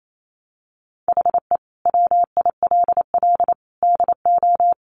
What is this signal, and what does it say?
Morse code, heard as a keyed tone, about 28 wpm, 700 Hz. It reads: HE WILL DO
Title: Sentences from Top 1000 Words 28wpm